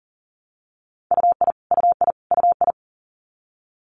Morse code, heard as a keyed tone, about 41 wpm, 700 Hz. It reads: UI UI UI
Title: Multi-letter Phonemes Repeated 3x 40wpm